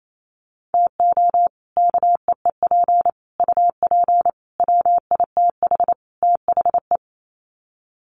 Morse code, heard as a keyed tone, about 28 wpm, 700 Hz. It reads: TO KEEP UP WITH THE
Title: Sets of 5 Words Encore 28wpm